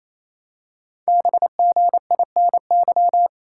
Morse code, heard as a keyed tone, about 27 wpm, 700 Hz. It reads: BGINY
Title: Code Groups - 5 Letters - 28wpm